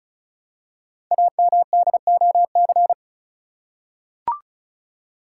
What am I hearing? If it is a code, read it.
Morse code, heard as a keyed tone, about 36 wpm, 700 Hz. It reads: AMDOC
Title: Code Groups - 5 Letters - 35wpm